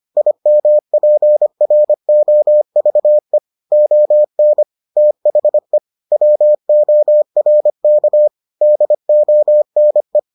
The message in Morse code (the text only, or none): IMPROVE ON THE WORK DONE